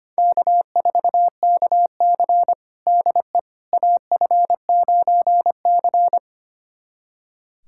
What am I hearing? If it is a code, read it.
Morse code, heard as a keyed tone, about 25 wpm, 700 Hz. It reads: K4KC DE AF9C
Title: USA Callsign DE Callsign - 4 Characters - 25wpm